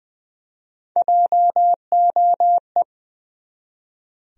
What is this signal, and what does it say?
Morse code, heard as a keyed tone, about 20 wpm, 700 Hz. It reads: JOE
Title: USA Common Names - Max 5 Letters - 20wpm